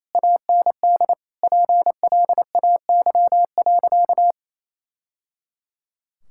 Morse code, heard as a keyed tone, about 28 wpm, 700 Hz. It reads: AND PLAY.
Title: Sentences from Top 1000 Words - Review Part 6 - 28wpm